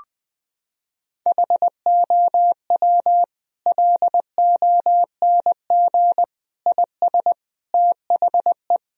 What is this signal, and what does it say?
Morse code, heard as a keyed tone, about 20 wpm, 700 Hz. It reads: HOW LONG IS THE
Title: Sentences from Top 1000 Words - Review Part 4 - 20wpm